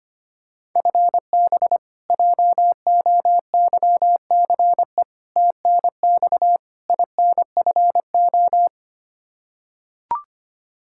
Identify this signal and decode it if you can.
Morse code, heard as a keyed tone, about 25 wpm, 700 Hz. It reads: FB JOYCE TNX INFO
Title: K1USN SST QSO Practice 25wpm